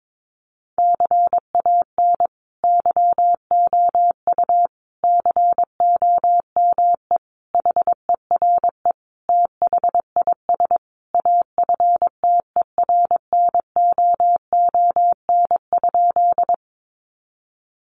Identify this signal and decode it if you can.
Morse code, heard as a keyed tone, about 22 wpm, 700 Hz. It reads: CAN YOU COME HERE THIS AFTERNOON?
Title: Sentences from Top 1500 Words 22wpm